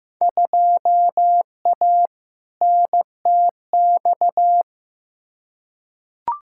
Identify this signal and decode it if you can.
Morse code, heard as a keyed tone, about 15 wpm, 700 Hz. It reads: 2A NTX
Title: Field Day Exchanges - 15wpm